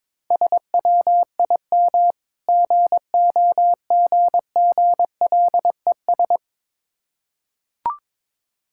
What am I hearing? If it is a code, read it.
Morse code, heard as a keyed tone, about 22 wpm, 690 Hz. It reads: SWIM GOGGLES
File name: Difficult Words with Hints - 22wpm